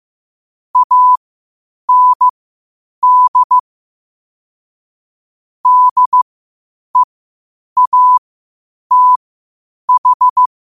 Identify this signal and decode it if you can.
Morse code, heard as a keyed tone, about 15 wpm, spaced as for 8 wpm, 1000 Hz. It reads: AND DEATH
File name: qotdfiles / 8wpm